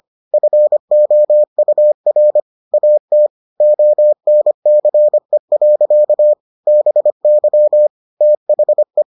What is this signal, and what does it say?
Morse code, heard as a keyed tone, about 25 wpm, 600 Hz. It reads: FOUR AT ONCE. BY THE